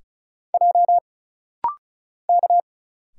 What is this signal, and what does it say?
Morse code, heard as a keyed tone, about 35 wpm, 700 Hz. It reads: J K
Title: Single Letters - Rapid-Fire 35wpm